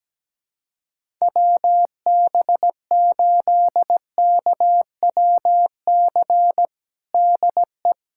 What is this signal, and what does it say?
Morse code, heard as a keyed tone, about 17 wpm, 700 Hz. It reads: WB8KWC DE
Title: USA Callsign DE Callsign - 6 Characters - 17wpm